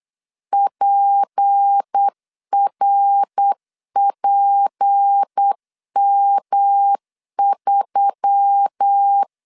Morse code, heard as a keyed tone, about 8 wpm, 800 Hz. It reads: PRPM3
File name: LEZIONI / 6 PKX0 / 6 P 8wpm 3025